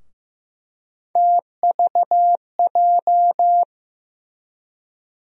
Morse code, heard as a keyed tone, about 15 wpm, 700 Hz. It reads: TVJ
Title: USA Call Sign Postfixes - 3 Letters 15wpm